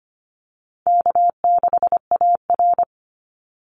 Morse code, heard as a keyed tone, about 25 wpm, 700 Hz. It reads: K6AR
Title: Call Signs Repeated 2x - 25wpm